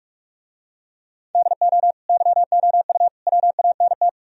Morse code, heard as a keyed tone, about 44 wpm, 700 Hz. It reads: DO YOU WANT